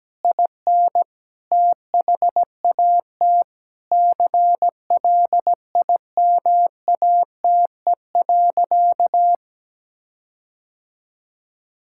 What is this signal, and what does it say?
Morse code, heard as a keyed tone, about 17 wpm, 700 Hz. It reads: IN THAT CLIMATE.
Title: Sentences from Top 1600 Words 17wpm